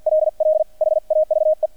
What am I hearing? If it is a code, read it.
Morse code, heard as a keyed tone, about 50 wpm, spaced as for 44 wpm, 620 Hz. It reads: LL HAVE
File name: archived / CRSnet